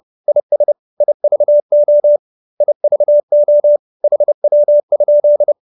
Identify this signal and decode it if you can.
Morse code, heard as a keyed tone, about 30 wpm, 570 Hz. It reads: IS IVO IVO HW?
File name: QSO319 30